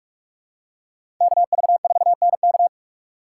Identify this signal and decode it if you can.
Morse code, heard as a keyed tone, about 46 wpm, 700 Hz. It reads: KV4NK
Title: USA Call Signs - 3x Character Spacing - 45wpm